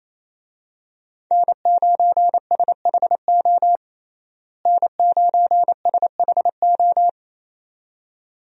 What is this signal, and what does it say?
Morse code, heard as a keyed tone, about 28 wpm, 700 Hz. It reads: N9SHO N9SHO
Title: USA Call Signs - 5 Characters - Repeated 2x - 28wpm